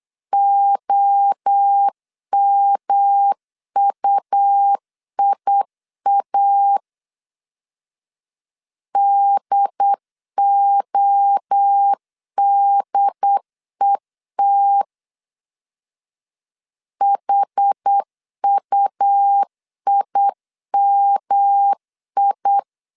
Morse code, heard as a keyed tone, about 8 wpm, 800 Hz. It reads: OMUIA DODET HUIMI